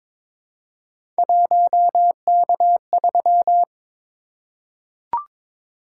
Morse code, heard as a keyed tone, about 22 wpm, 700 Hz. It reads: 1K3
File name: practice / 22wpm / Three Letter-Number Combinations - 22wpm